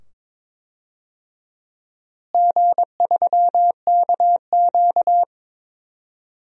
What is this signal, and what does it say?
Morse code, heard as a keyed tone, about 22 wpm, 700 Hz. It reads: G3KQ
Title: DXCC-100 Call Signs - 4 Characters - Repeated 2x - 22wpm